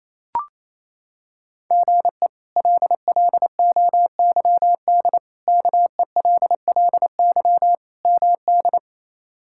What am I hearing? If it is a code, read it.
Morse code, heard as a keyed tone, about 27 wpm, 700 Hz. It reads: GE LLOYD KELLY MD
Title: K1USN SST QSO Practice - 28wpm